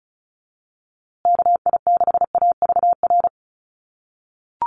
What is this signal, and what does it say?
Morse code, heard as a keyed tone, about 34 wpm, 700 Hz. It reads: KI6AVR
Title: USA Call Signs Repeated 2x 35wpm